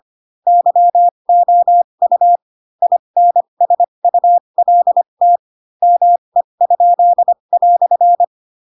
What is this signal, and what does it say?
Morse code, heard as a keyed tone, about 25 wpm, spaced as for 22 wpm, 700 Hz. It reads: YOU INSULT ME?"
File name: Pinocchio / Pinocchio ch2 18 25 ik1hge